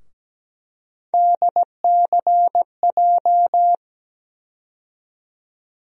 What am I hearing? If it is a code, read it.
Morse code, heard as a keyed tone, about 17 wpm, 700 Hz. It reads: DCJ
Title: USA Call Sign Postfixes - 3 Letters 17wpm